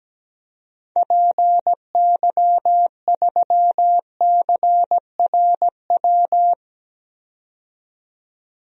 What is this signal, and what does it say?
Morse code, heard as a keyed tone, about 17 wpm, 700 Hz. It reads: PY3CRW
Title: DXCC-100 Call Signs - 6 Characters - 17wpm